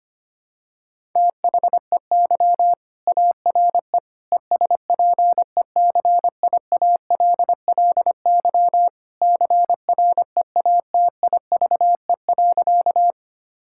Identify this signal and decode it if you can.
Morse code, heard as a keyed tone, about 25 wpm, 700 Hz. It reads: THEY ARE ESPECIALLY CREATIVE.
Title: Sentences from Top 1000 Words - Review Part 5 - 25wpm